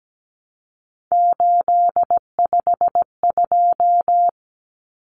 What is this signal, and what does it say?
Morse code, heard as a keyed tone, about 17 wpm, 700 Hz. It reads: 852
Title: Common Cut Numbers - 3-Digit 17wpm